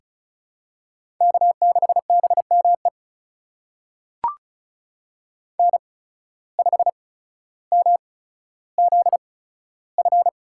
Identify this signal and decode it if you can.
Morse code, heard as a keyed tone, about 36 wpm, spaced as for 15 wpm, 700 Hz. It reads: K6BME N5MZF
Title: USA Call Signs - 5 Characters - 8x Character Spacing - 35wpm